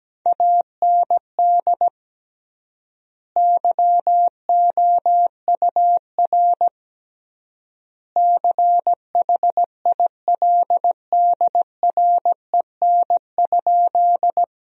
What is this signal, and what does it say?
Morse code, heard as a keyed tone, about 17 wpm, 700 Hz. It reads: AND YOUR CHILDREN?
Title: 3x Word-Spacing - Sentences from Top 300 Words - 17wpm